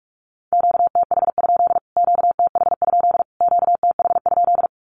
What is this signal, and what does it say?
Morse code, heard as a keyed tone, about 45 wpm, 700 Hz. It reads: QTH? QTH? QTH?
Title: Q-Codes Repeated 3x 45wpm